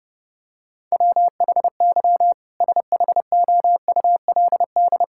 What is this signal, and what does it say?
Morse code, heard as a keyed tone, about 30 wpm, 700 Hz. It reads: WHY SHOULD
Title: Sentences from Top 1600 Words 30wpm